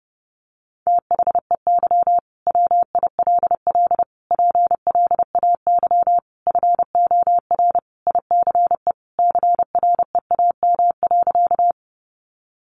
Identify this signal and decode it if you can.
Morse code, heard as a keyed tone, about 30 wpm, 700 Hz. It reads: THEY WILL PLAY FOR ICE CREAM.
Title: Sentences from Top 1500 Words 30wpm